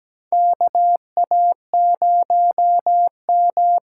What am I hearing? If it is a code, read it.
Morse code, heard as a keyed tone, about 17 wpm, 700 Hz. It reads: KA0M
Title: USA Call Signs - 6 Characters - 8x Character Spacing - 17wpm